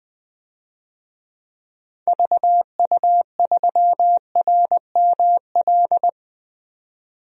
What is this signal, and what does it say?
Morse code, heard as a keyed tone, about 20 wpm, 700 Hz. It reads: VU3RML
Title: DXCC-100 Call Signs - 6 Characters - 20wpm